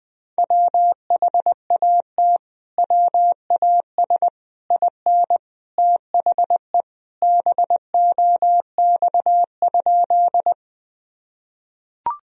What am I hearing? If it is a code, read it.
Morse code, heard as a keyed tone, about 20 wpm, 700 Hz. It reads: WHAT WAS IN THE BOX?
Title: Sentences from Top 1000 Words - Review Part 2 - 20wpm